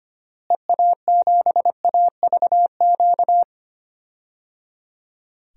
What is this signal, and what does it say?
Morse code, heard as a keyed tone, about 25 wpm, 700 Hz. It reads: EA7AVQ
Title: DXCC-100 Call Signs - 6 Characters - Repeated 3x - 25wpm